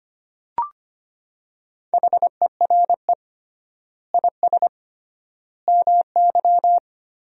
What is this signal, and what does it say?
Morse code, heard as a keyed tone, about 25 wpm, 700 Hz. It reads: HERE IS MY
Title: 3x Word-Spacing - Sentences from Top 200 Words - 25wpm